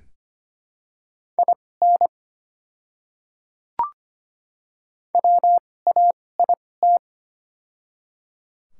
Morse code, heard as a keyed tone, about 25 wpm, spaced as for 18 wpm, 690 Hz. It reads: IN WAIT
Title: 2x Character Spacing - Top 500 Words - 25wpm